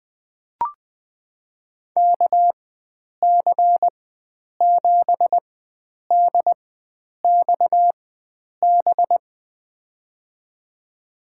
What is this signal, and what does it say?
Morse code, heard as a keyed tone, about 21 wpm, spaced as for 9 wpm, 700 Hz. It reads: KC7DXB
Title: USA Call Signs - 4x Character Spacing - 20wpm